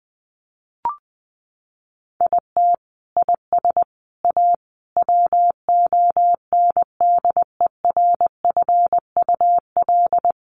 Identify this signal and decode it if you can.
Morse code, heard as a keyed tone, about 20 wpm, 700 Hz. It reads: IT IS A WONDERFUL
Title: Sentences from Top 1300 Words 20wpm